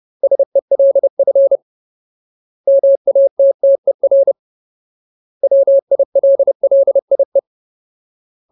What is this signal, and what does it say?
Morse code, heard as a keyed tone, about 30 wpm, 550 Hz. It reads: SELF MATTER WILLIE
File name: Ultimate Word List / 30WPM Ultimate